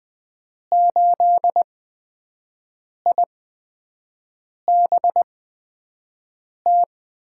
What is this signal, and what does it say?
Morse code, heard as a keyed tone, about 20 wpm, spaced as for 5 wpm, 700 Hz. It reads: 8IBT
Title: USA Call Signs - 6 Characters - 8x Character Spacing - 20wpm